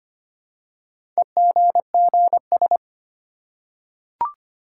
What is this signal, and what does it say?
Morse code, heard as a keyed tone, about 25 wpm, 700 Hz. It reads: EGGS
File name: practice / 25wpm / Easy Words with Hints - 25wpm